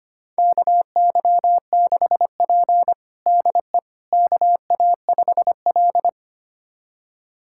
Morse code, heard as a keyed tone, about 25 wpm, 700 Hz. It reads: KY6P DE KA5L
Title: USA Callsign DE Callsign - 4 Characters - 25wpm